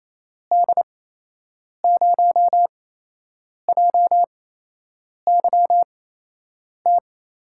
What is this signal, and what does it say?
Morse code, heard as a keyed tone, about 28 wpm, spaced as for 8 wpm, 700 Hz. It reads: D0JYT
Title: USA Call Signs - 8x Character Spacing - 28wpm